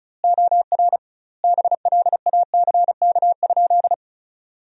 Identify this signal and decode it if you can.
Morse code, heard as a keyed tone, about 36 wpm, 700 Hz. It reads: OR BLACK?
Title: 2x Word-Spacing - Sentences from Top 400 Words - 35wpm